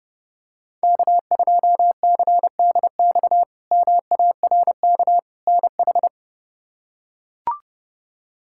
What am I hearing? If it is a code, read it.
Morse code, heard as a keyed tone, about 30 wpm, 700 Hz. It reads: K2CDX MARK NH
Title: K1USN SST QSO Practice - 30wpm